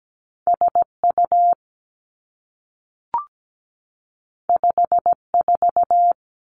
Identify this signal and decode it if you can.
Morse code, heard as a keyed tone, about 17 wpm, 700 Hz. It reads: SU 54
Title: Two Challenging Letter-Number Combinations 17wpm